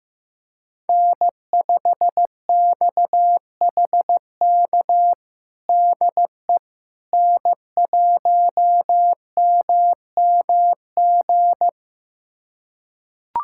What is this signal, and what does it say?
Morse code, heard as a keyed tone, about 15 wpm, 700 Hz. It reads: N5XHK DE N1MMG
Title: USA Callsign DE Callsign - 5 Characters - 15wpm